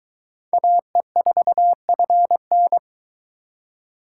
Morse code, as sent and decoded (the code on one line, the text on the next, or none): .- . ....- ..-. -.
AE4FN